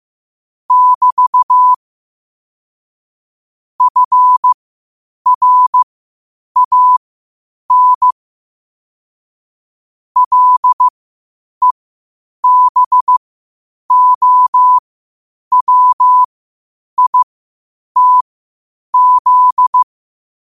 Morse code, heard as a keyed tone, about 15 wpm, spaced as for 8 wpm, 1000 Hz. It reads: <BT> FRAN LEBOWITZ
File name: qotdfiles / 8wpm